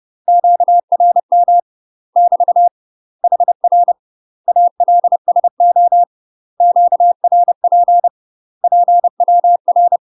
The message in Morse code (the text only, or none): QRM <BT> HR ALSO QRP PWR